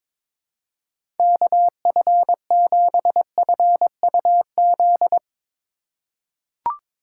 Morse code, heard as a keyed tone, about 22 wpm, 700 Hz. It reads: KF7FUZ
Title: USA Call Signs - 6 Characters - 3x Character Spacing - 22wpm